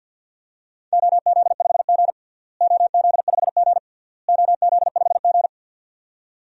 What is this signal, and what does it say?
Morse code, heard as a keyed tone, about 52 wpm, 700 Hz. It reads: OZ5G OZ5G OZ5G
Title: DXCC-100 Call Signs - 4 Characters - Repeated 3x - 50wpm